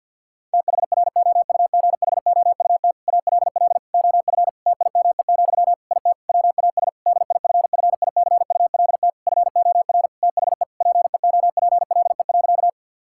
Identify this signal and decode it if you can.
Morse code, heard as a keyed tone, about 50 wpm, 700 Hz. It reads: THROUGHOUT ALL OF TIME, IT WAS DIFFICULT FOR THE PEOPLE.
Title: Sentences from Top 1000 Words - Review Part 5 - 50wpm